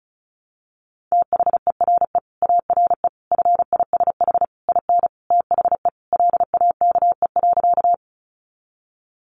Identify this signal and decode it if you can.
Morse code, heard as a keyed tone, about 34 wpm, 700 Hz. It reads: THERE ARE FISH IN THE LAKE.
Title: Sentences from Top 1100 Words 35wpm